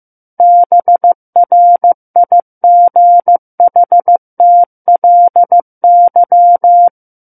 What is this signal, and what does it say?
Morse code, heard as a keyed tone, about 15 wpm, 700 Hz. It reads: BRIGHTLY
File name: Book One / 06-Chapter 6 - 15 WPM